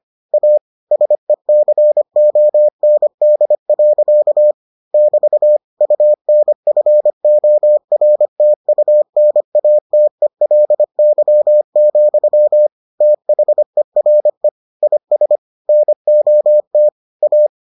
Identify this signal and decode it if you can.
Morse code, heard as a keyed tone, about 25 wpm, 600 Hz. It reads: A SECOND. <BT> UNFORTUNATELY, THERE IS NOT A